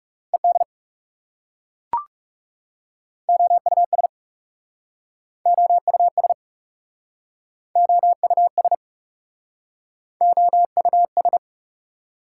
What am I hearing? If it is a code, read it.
Morse code, heard as a keyed tone, about 40 wpm, 690 Hz. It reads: ED OUS OUS OUS OUS